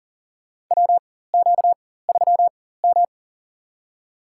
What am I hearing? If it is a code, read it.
Morse code, heard as a keyed tone, about 40 wpm, spaced as for 19 wpm, 700 Hz. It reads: WQ3M
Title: USA Call Signs - 4 Characters - 4x Character Spacing - 40wpm